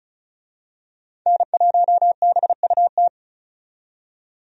Morse code, heard as a keyed tone, about 36 wpm, 700 Hz. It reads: N1BUT
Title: USA Call Signs - 5 Characters - 35wpm